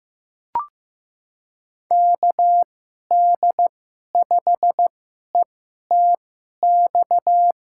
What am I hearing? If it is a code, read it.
Morse code, heard as a keyed tone, about 15 wpm, spaced as for 11 wpm, 700 Hz. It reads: KD5ETX
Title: USA Call Signs - 6 Characters - 2x Character Spacing - 15wpm